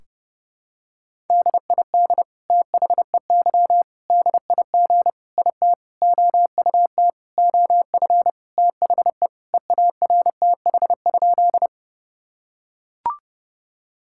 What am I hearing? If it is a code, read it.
Morse code, heard as a keyed tone, about 30 wpm, 700 Hz. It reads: DID THEY DIG IT OUT OF THE EARTH?
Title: Sentences from Top 1600 Words 30wpm